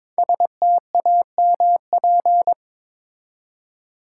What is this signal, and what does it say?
Morse code, heard as a keyed tone, about 22 wpm, 700 Hz. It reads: STAMP
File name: 5 Letter Words - First 3 Letters Given - 22wpm